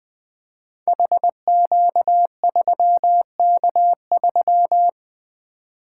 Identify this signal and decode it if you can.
Morse code, heard as a keyed tone, about 20 wpm, 700 Hz. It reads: HQ3K3
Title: Code Groups - 5 Letter:Numbers - 20wpm